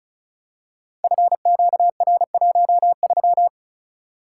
Morse code, heard as a keyed tone, about 36 wpm, 700 Hz. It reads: FQR13
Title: Code Groups - 5 Letter:Numbers - 35wpm